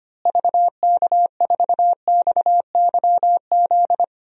Morse code, heard as a keyed tone, about 25 wpm, 700 Hz. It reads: VK4XYZ
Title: DXCC-100 Call Signs - 6 Characters - Repeated 3x - 25wpm